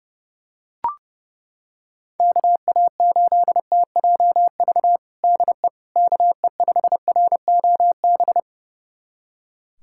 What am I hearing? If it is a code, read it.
Morse code, heard as a keyed tone, about 30 wpm, 700 Hz. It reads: KA8TJV DE KE5ROB
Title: USA Callsign DE Callsign 30wpm